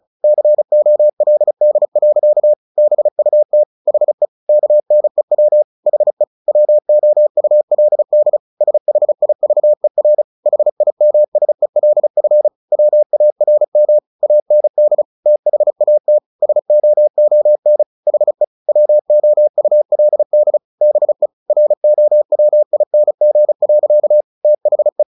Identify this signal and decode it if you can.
Morse code, heard as a keyed tone, about 35 wpm, 600 Hz. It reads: COLD. BUT HE KNEW HE WOULD SHIVER HIMSELF WARM AND THAT SOON HE WOULD BE ROWING. THE
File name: the old man and the sea / the old man and the sea 1 35wpm